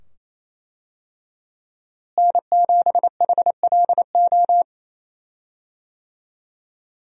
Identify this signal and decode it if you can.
Morse code, heard as a keyed tone, about 27 wpm, 700 Hz. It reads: N7HLO
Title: USA Call Signs - 5 Characters - Repeated 2x - 28wpm